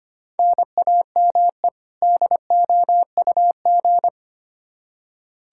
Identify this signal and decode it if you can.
Morse code, heard as a keyed tone, about 25 wpm, 700 Hz. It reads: NAME DOUG
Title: Practice QSO - Exchange 1 - Beginning - 25wpm